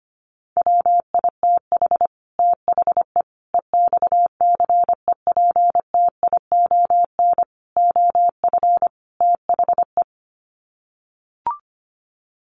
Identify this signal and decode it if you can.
Morse code, heard as a keyed tone, about 25 wpm, 700 Hz. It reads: WITH THE EXCEPTION OF THE
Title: Sets of 5 Words Encore 25wpm